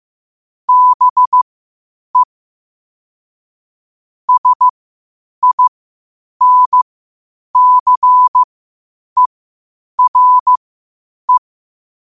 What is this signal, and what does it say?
Morse code, heard as a keyed tone, about 15 wpm, spaced as for 8 wpm, 1000 Hz. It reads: BE SINCERE
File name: qotdfiles / 8wpm